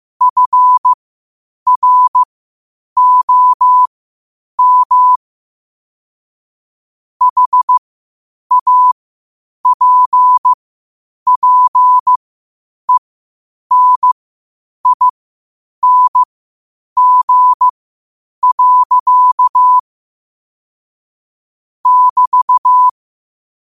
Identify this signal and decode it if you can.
Morse code, heard as a keyed tone, about 15 wpm, spaced as for 8 wpm, 1000 Hz. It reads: FROM HAPPENING. <BT>
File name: qotdfiles / 8wpm